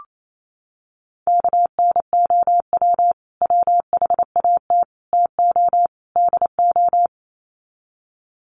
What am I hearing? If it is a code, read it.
Morse code, heard as a keyed tone, about 28 wpm, 700 Hz. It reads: KNOW WHAT TO DO
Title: Sets of 4 Words 28wpm